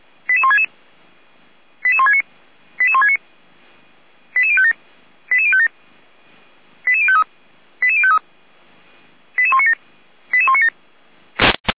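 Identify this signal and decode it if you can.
Five-tone selective call (ZVEI), heard as quick runs of five tones, tones of about 70 ms. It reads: 88166 88169 88169 88057 88057 88053 88053 88197 88197
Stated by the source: ZVEI five-tone selective call received on an SDR.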